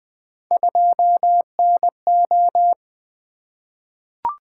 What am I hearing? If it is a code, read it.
Morse code, heard as a keyed tone, about 20 wpm, 700 Hz. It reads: 2NO
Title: Three Letter-Number Combinations - 20wpm